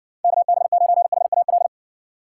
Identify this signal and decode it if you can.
Morse code, heard as a keyed tone, about 60 wpm, 700 Hz. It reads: KB9HAL